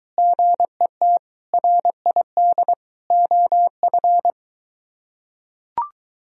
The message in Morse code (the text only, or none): GET RID OF